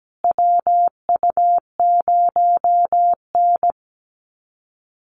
Morse code, heard as a keyed tone, about 17 wpm, 700 Hz. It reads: WU0N
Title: USA Call Signs Repeated 2x 17wpm